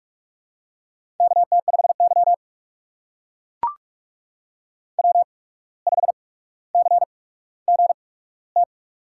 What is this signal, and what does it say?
Morse code, heard as a keyed tone, about 44 wpm, spaced as for 18 wpm, 700 Hz. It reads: KT5Y W5CGT
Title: USA Call Signs - 8x Character Spacing - 45wpm